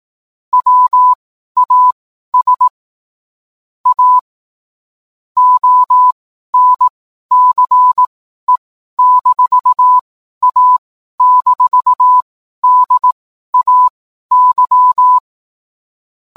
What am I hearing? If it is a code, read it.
Morse code, heard as a keyed tone, about 18 wpm, spaced as for 12 wpm, 1000 Hz. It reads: WAS A ONCE-A-DAY
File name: cwblog / cw mp3 directory / mem morse 12